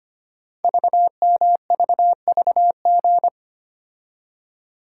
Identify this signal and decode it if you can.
Morse code, heard as a keyed tone, about 25 wpm, 700 Hz. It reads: VMVVG
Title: Code Groups - 5 Letters - 25wpm